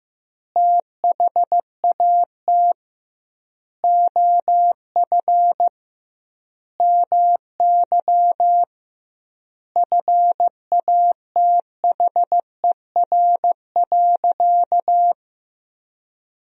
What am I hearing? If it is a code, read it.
Morse code, heard as a keyed tone, about 15 wpm, 700 Hz. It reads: THAT OF MY FATHER.
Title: 2x Word-Spacing - Sentences from Top 500 Words - 15wpm